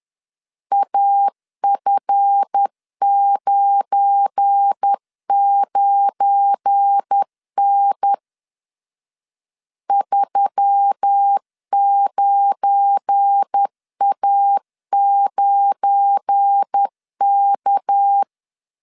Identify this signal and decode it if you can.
Morse code, heard as a keyed tone, about 10 wpm, 800 Hz. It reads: AF99N 39A9K
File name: LEZIONI / 7 GF9Z / 7 GF9 10wpm 2747